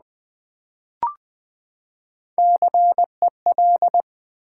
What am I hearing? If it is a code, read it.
Morse code, heard as a keyed tone, about 20 wpm, 700 Hz. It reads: CEL
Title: Build A Word - 3 Letters At A Time - 9-Letter Words - 20wpm